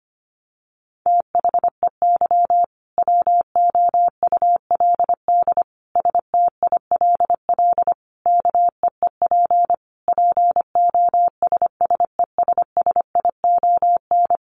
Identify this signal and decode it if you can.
Morse code, heard as a keyed tone, about 25 wpm, 700 Hz. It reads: THEY WOULD STILL KEEP POSSESSION
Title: Sentences from Top 1400 Words 25wpm